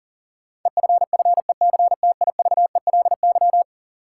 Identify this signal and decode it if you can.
Morse code, heard as a keyed tone, about 37 wpm, 700 Hz. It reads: EFFECTIVELY
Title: Sentences from Top 1400 Words - New Word Drill - 40wpm